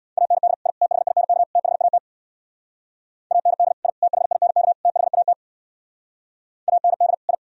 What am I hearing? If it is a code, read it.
Morse code, heard as a keyed tone, about 62 wpm, 700 Hz. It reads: AND I THINK THAT AND I THINK THAT AND I
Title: Speed-Racing - Sets of 4 Words 50wpm